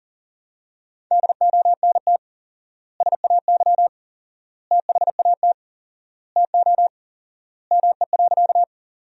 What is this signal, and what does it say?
Morse code, heard as a keyed tone, about 37 wpm, 700 Hz. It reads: DONT SAY THAT TO ME.
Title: 4x Word-Spacing - Sentences from Top 300 Words - 40wpm